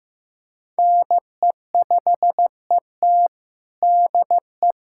Morse code, heard as a keyed tone, about 15 wpm, 700 Hz. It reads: NE5ET DE
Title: USA Callsign DE Callsign - 5 Characters - 15wpm